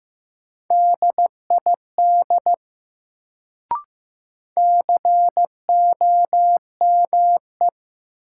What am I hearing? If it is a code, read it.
Morse code, heard as a keyed tone, about 15 wpm, 700 Hz. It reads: DID COME
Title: Sentences from Top 100 Words - New Word Drill - 15wpm